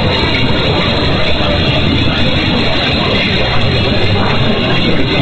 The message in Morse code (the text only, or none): none